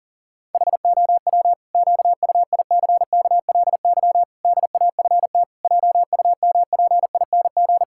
Morse code, heard as a keyed tone, about 40 wpm, 700 Hz. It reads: HOW QUICKLY DAFT JUMPING
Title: Pangram Challenge v2 - 40wpm